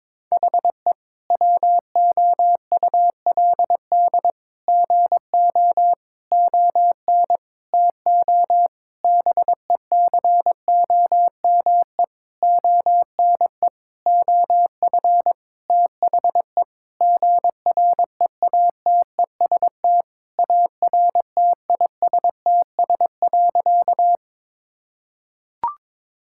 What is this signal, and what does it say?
Morse code, heard as a keyed tone, about 22 wpm, 700 Hz. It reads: HE WOULD GO ON TO BECOME ONE OF THE GREATEST ARTISTS.
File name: practice / 22wpm / Sentences with 8-Word Phrases - 22wpm